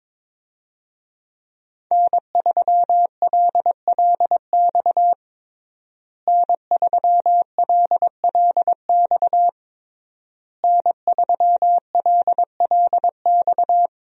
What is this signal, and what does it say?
Morse code, heard as a keyed tone, about 22 wpm, 700 Hz. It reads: N3LLX N3LLX N3LLX
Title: USA Call Signs - 5 Characters - Repeated 3x - 22wpm